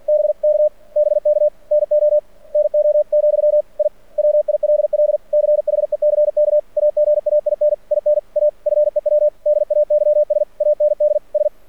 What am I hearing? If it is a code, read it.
Morse code, heard as a keyed tone, about 47 wpm, 600 Hz. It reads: BK BK NO NO, I WILL CHECK AGAIN IN A FEW DAYS AND S
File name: archived / CRSnet